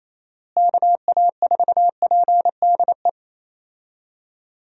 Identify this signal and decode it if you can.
Morse code, heard as a keyed tone, about 28 wpm, 700 Hz. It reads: KA4PDE
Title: USA Call Signs - 6 Characters - Repeated 2x - 28wpm